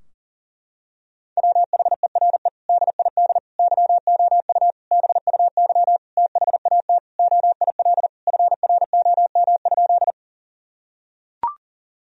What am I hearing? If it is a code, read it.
Morse code, heard as a keyed tone, about 40 wpm, 700 Hz. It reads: WHERE DID YOU BUY THAT OIL FROM?
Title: Sentences from Top 1000 Words - Review Part 4 - 40wpm